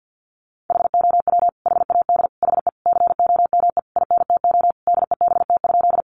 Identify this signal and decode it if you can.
Morse code, heard as a keyed tone, about 50 wpm, 700 Hz. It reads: HOW HAD HE COME INTO DEBT?
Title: Sentences from Top 1300 Words 50wpm